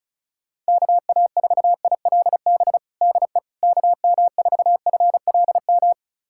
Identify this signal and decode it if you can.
Morse code, heard as a keyed tone, about 36 wpm, 700 Hz. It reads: KA4ILB DE KM4FLM
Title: USA Callsign DE Callsign 35wpm